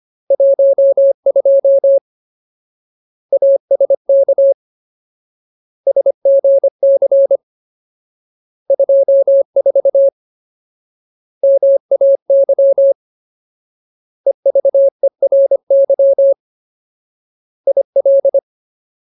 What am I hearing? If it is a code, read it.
Morse code, heard as a keyed tone, about 25 wpm, 550 Hz. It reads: 12 ASK SGC 24 MAY EVERY IL
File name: Ultimate Word List / 25WPM Ultimate